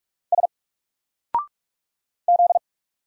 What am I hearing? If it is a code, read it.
Morse code, heard as a keyed tone, about 44 wpm, 700 Hz. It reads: S Z
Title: Single Character (Letter:Number:Punctuation:Prosign) Alt 45wpm